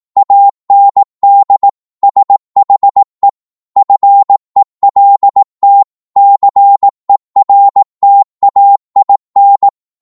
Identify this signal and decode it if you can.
Morse code, heard as a keyed tone, about 18 wpm, 800 Hz. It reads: AND SHE FELT CERTAIN
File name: Morse / psg-18wpm